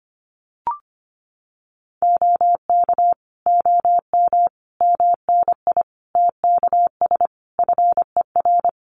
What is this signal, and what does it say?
Morse code, heard as a keyed tone, about 25 wpm, 700 Hz. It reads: OK OM MNI TKS FER
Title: Common QSO Phrases 25wpm